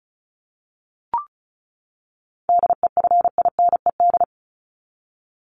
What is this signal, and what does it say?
Morse code, heard as a keyed tone, about 34 wpm, 700 Hz. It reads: DEFINED
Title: ED Suffix - Top 100 Words with Base Words 35wpm